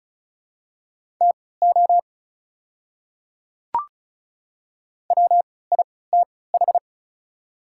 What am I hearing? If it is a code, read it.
Morse code, heard as a keyed tone, about 36 wpm, spaced as for 20 wpm, 700 Hz. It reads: TO WITH
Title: 3x Character Spacing - Top 100 Words - 35wpm